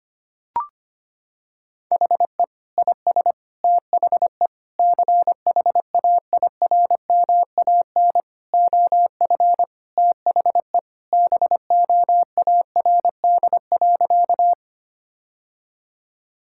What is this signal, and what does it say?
Morse code, heard as a keyed tone, about 25 wpm, 700 Hz. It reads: HE IS THE CHAIRMAN OF THE BOARD.
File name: practice / 25wpm / Sentences with 3-Word Phrases - 25wpm